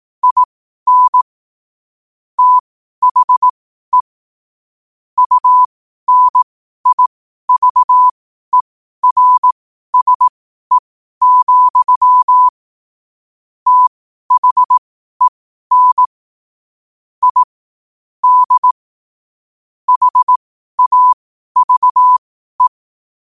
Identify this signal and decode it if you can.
Morse code, heard as a keyed tone, about 18 wpm, spaced as for 12 wpm, 1000 Hz. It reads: IN THE UNIVERSE, THEN I D HAVE
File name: qotdfiles / 12wpm